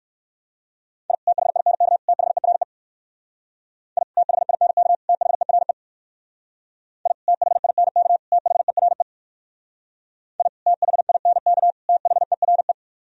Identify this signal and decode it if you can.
Morse code, heard as a keyed tone, about 57 wpm, 700 Hz. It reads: I THINK THERE I THINK THERE I THINK THERE I THINK THERE